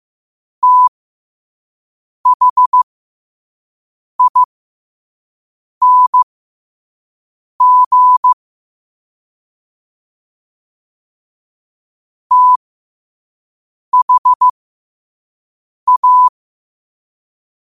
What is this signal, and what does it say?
Morse code, heard as a keyed tone, about 16 wpm, spaced as for 5 wpm, 1000 Hz. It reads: THING THA
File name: qotdfiles / 5wpm